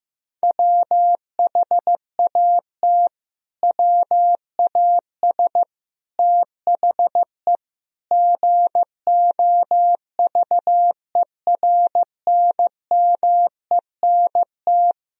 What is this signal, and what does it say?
Morse code, heard as a keyed tone, about 15 wpm, 700 Hz. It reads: WHAT WAS THE GOVERNMENT
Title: Sentences from Top 1000 Words - Review Part 3 - 15wpm